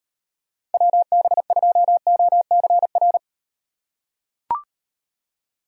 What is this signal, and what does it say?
Morse code, heard as a keyed tone, about 37 wpm, 700 Hz. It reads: WB2OCR